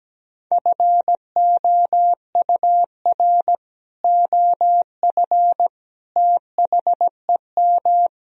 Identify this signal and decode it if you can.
Morse code, heard as a keyed tone, about 17 wpm, 700 Hz. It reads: FOUR OF THEM
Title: Sentences from Top 1000 Words - Review Part 1 - 17wpm